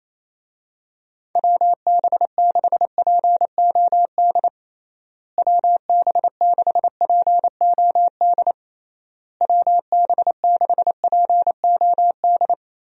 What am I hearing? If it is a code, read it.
Morse code, heard as a keyed tone, about 28 wpm, 700 Hz. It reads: WB6POD WB6POD WB6POD
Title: USA Call Signs - 6 Characters - Repeated 3x - 28wpm